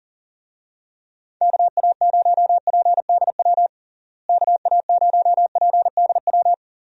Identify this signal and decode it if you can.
Morse code, heard as a keyed tone, about 40 wpm, 700 Hz. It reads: KA0PDW KA0PDW
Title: USA Call Signs - 6 Characters - Repeated 3x - 40wpm